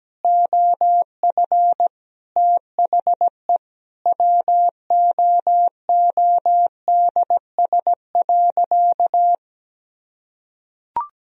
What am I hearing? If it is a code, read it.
Morse code, heard as a keyed tone, about 17 wpm, 700 Hz. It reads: OF THE WOODS.
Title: Sentences from Top 1000 Words - Review Part 2 - 17wpm